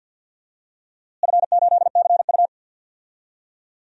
Morse code, heard as a keyed tone, about 52 wpm, 700 Hz. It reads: F8CU
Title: DXCC-100 Call Signs - 4 Characters - Repeated 2x - 50wpm